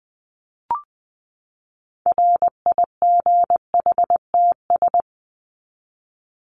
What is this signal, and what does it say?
Morse code, heard as a keyed tone, about 20 wpm, 700 Hz. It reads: RIGHTS
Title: ES:S Suffix - Top 100 Words with Base Words 20wpm